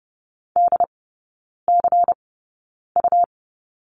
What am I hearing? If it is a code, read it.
Morse code, heard as a keyed tone, about 30 wpm, spaced as for 9 wpm, 700 Hz. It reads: DCU
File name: Lesson 18e - Letter m - Three Characters 30wpm